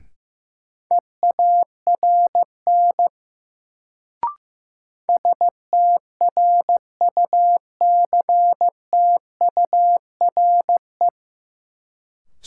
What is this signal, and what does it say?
Morse code, heard as a keyed tone, about 15 wpm, 700 Hz. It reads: EARN STRUCTURE
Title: Sentences from Top 1500 Words - New Word Drill Alt - 15wpm